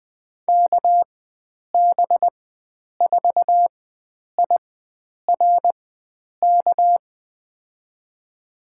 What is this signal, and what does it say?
Morse code, heard as a keyed tone, about 20 wpm, spaced as for 9 wpm, 700 Hz. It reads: KB4IRK
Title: USA Call Signs - 6 Characters - 4x Character Spacing - 20wpm